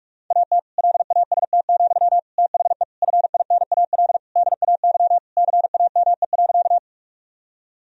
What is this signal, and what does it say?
Morse code, heard as a keyed tone, about 46 wpm, 700 Hz. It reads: AT LAST, THE FINAL DAY CAME.
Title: Sentences from Top 1000 Words - Review Part 2 - 45wpm